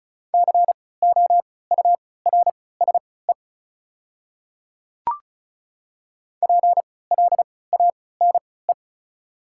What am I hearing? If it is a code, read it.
Morse code, heard as a keyed tone, about 34 wpm, spaced as for 20 wpm, 700 Hz. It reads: COURSE PLANE
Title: 3x Character Spacing - Top 500 Words - 35wpm